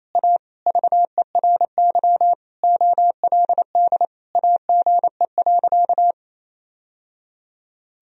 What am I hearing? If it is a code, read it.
Morse code, heard as a keyed tone, about 28 wpm, 700 Hz. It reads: A VERY OLD AGE.
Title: Sentences from Top 1000 Words - Review Part 3 - 28wpm